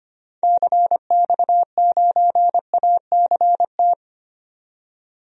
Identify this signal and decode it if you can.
Morse code, heard as a keyed tone, about 25 wpm, 700 Hz. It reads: CX9ACT
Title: DXCC-100 Call Signs - 6 Characters - 25wpm